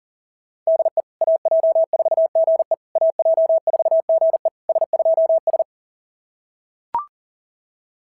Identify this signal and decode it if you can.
Morse code, heard as a keyed tone, about 37 wpm, 690 Hz. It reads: DE AJ4GE AJ4GE S2S
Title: SOTA QSO Practice 40wpm